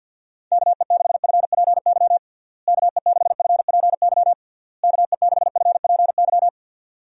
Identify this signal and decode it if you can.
Morse code, heard as a keyed tone, about 49 wpm, 700 Hz. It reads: KE6FPY KE6FPY KE6FPY
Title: USA Call Signs - 6 Characters - Repeated 3x - 50wpm